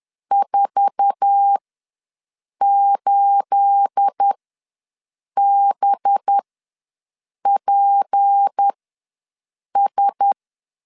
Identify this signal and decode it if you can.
Morse code, heard as a keyed tone, about 10 wpm, spaced as for 6 wpm, 800 Hz. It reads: 48BPS
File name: LEZIONI / 9 28 ws 10wpm 4124